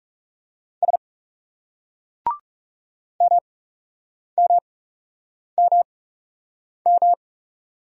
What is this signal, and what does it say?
Morse code, heard as a keyed tone, about 37 wpm, spaced as for 8 wpm, 700 Hz. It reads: S MMMM
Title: Speed-Racing Single Letter-Number 30wpm